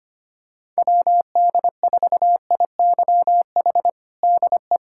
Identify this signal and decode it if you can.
Morse code, heard as a keyed tone, about 25 wpm, 700 Hz. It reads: WD4IYH DE
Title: Practice QSO - Exchange 4 - Ending - 25wpm